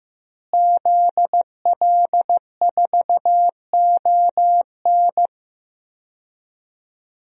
Morse code, heard as a keyed tone, about 16 wpm, 700 Hz. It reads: ZL4ON
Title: DXCC-100 Call Signs - 5 Characters - Repeated 3x - 15wpm